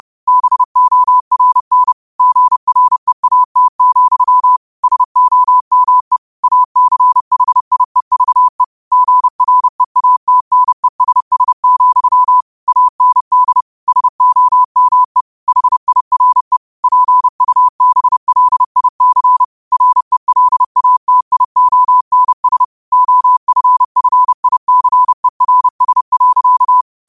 Morse code, heard as a keyed tone, about 30 wpm, 1000 Hz. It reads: BORN GREAT, SOME ACHIEVE GREATNESS, AND SOME HIRE PUBLIC RELATIONS OFFICERS.
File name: 30wpm